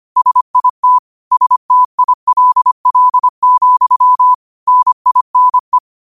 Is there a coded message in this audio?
Yes: Morse code from 0.2 to 5.8 s, heard as a keyed tone, about 25 wpm, 1000 Hz. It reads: SIT STILL, NINE